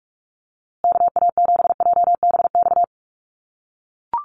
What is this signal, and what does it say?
Morse code, heard as a keyed tone, about 44 wpm, 690 Hz. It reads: KA7JBX
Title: USA Call Signs Repeated 3x 45wpm